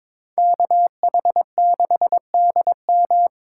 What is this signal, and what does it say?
Morse code, heard as a keyed tone, about 26 wpm, 700 Hz. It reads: KH6DM
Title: Speed Racing - USA 2x2 Call Signs 22wpm